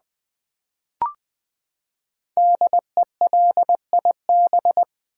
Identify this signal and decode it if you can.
Morse code, heard as a keyed tone, about 20 wpm, 700 Hz. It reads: DELIB
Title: Build A Word - 5 Letters At A Time - 10-Letter Words - 20wpm